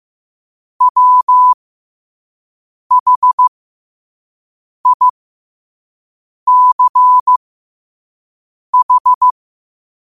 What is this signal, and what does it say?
Morse code, heard as a keyed tone, about 15 wpm, spaced as for 5 wpm, 1000 Hz. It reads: WHICH
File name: qotdfiles / 5wpm